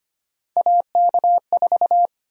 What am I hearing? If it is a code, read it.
Morse code, heard as a keyed tone, about 25 wpm, 700 Hz. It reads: AK4
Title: USA Call Sign Prefixes - Repeated 2x - 25wpm